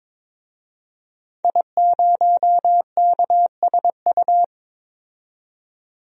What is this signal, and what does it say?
Morse code, heard as a keyed tone, about 22 wpm, 700 Hz. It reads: I0KSU
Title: DXCC-100 Call Signs - 5 Characters - 22wpm